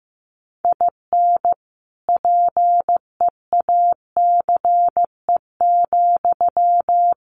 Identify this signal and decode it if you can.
Morse code, heard as a keyed tone, about 15 wpm, 700 Hz. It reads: IN PEACE,
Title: Sentences from Top 1500 Words 15wpm